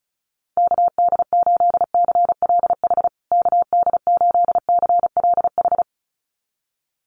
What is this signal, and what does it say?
Morse code, heard as a keyed tone, about 34 wpm, 700 Hz. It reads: KD8CLH KD8CLH
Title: USA Call Signs Repeated 2x 35wpm